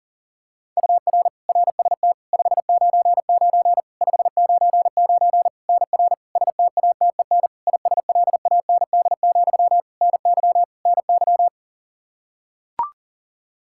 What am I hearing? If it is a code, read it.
Morse code, heard as a keyed tone, about 40 wpm, 700 Hz. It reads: UR RST 599 599 NR STATEN ISLAND, NY NY
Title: Practice QSO - Exchange 1 - Beginning - 40wpm